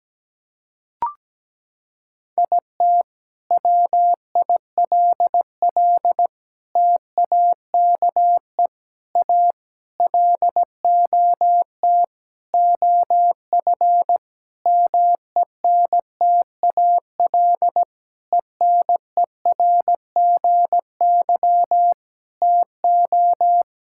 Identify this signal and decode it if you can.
Morse code, heard as a keyed tone, about 17 wpm, 700 Hz. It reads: IT WILL TAKE A LOT OF MENTAL ENERGY TO
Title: Sentences from Top 1000 Words - Review Part 6 - 17wpm